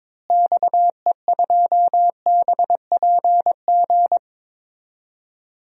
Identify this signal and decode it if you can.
Morse code, heard as a keyed tone, about 22 wpm, 700 Hz. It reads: XE2BPG
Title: DXCC-100 Call Signs - 6 Characters - Repeated 3x - 22wpm